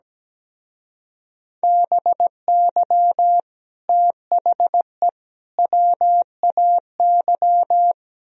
Morse code, heard as a keyed tone, about 17 wpm, 700 Hz. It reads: BY THE WAY
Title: Speed-Racing - Sets of 3 Words 17wpm